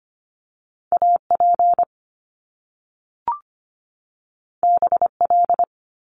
Morse code, heard as a keyed tone, about 25 wpm, 700 Hz. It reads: AP BL
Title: Top Two Letter Combinations in Words 25wpm